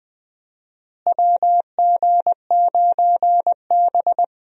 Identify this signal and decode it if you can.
Morse code, heard as a keyed tone, about 20 wpm, 700 Hz. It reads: WG9B
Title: USA Call Signs - 4 Characters - Repeated 2x - 20wpm